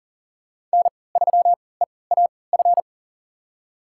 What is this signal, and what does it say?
Morse code, heard as a keyed tone, about 38 wpm, spaced as for 22 wpm, 690 Hz. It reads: N3EAF
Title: USA Call Signs - 5 Characters - 3x Character Spacing - 40wpm